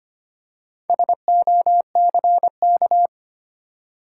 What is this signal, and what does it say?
Morse code, heard as a keyed tone, about 25 wpm, 700 Hz. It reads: SOCK
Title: Four Letter Words - 25wpm